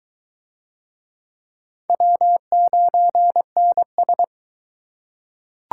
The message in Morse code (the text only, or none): W9NS